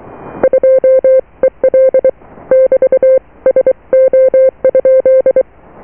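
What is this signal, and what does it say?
Morse code, heard as a keyed tone, about 23 wpm, 520 Hz. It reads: 2EL <BT> SO?
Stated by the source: Morse QSO on 14 MHz with a German station, received at RST 579.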